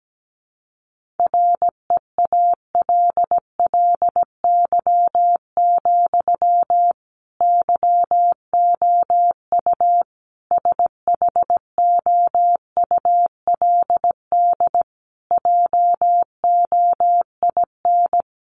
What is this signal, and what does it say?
Morse code, heard as a keyed tone, about 17 wpm, 700 Hz. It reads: REALLY, YOU SHOULD JOIN
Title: Sentences from Top 1400 Words 17wpm